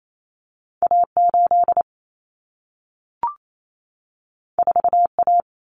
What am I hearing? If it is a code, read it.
Morse code, heard as a keyed tone, about 28 wpm, 700 Hz. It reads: A8 4A
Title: Common Cut Numbers - 2-Digit 28wpm